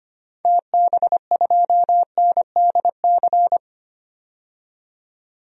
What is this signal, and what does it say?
Morse code, heard as a keyed tone, about 25 wpm, 700 Hz. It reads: TB2NDC
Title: DXCC-100 Call Signs - 6 Characters - 25wpm